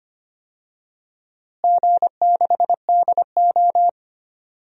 Morse code, heard as a keyed tone, about 25 wpm, 700 Hz. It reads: G6DO
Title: DXCC-100 Call Signs - 4 Characters - Repeated 3x - 25wpm